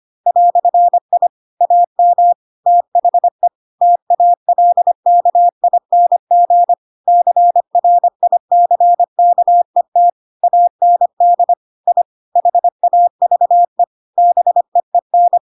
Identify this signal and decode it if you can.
Morse code, heard as a keyed tone, about 25 wpm, 700 Hz. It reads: "I AM THE TALKING CRICKET AND I HAVE BEEN
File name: Pinocchio / Pinocchio ch4 25 25 ik1hge